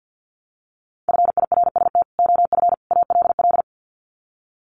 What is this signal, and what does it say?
Morse code, heard as a keyed tone, about 50 wpm, 700 Hz. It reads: FIRST OF ALL
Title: Sets of 3 Words 50wpm